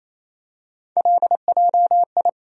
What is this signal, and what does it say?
Morse code, heard as a keyed tone, about 28 wpm, 700 Hz. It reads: LJI
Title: USA Call Sign Postfixes - 3 Letters 28wpm